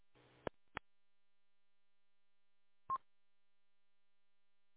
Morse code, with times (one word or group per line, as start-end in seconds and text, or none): none